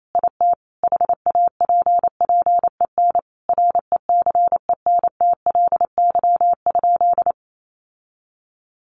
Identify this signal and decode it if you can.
Morse code, heard as a keyed tone, about 28 wpm, 700 Hz. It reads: IT HAPPEN RECENTLY?
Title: Sentences from Top 1200 Words 28wpm